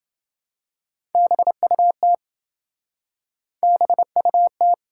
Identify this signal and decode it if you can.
Morse code, heard as a keyed tone, about 28 wpm, 700 Hz. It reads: BUT BUT